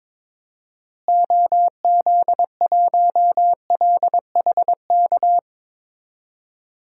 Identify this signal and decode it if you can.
Morse code, heard as a keyed tone, about 22 wpm, 700 Hz. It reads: OZ1LHK
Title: DXCC-100 Call Signs - 6 Characters - Repeated 2x - 22wpm